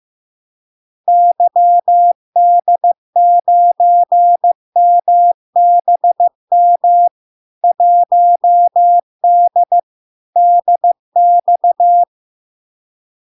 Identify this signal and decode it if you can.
Morse code, heard as a keyed tone, about 15 wpm, 700 Hz. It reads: YD9MBM 1D DX
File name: LARK-FD-2022-Log-in-CW-15wpm-700-Hz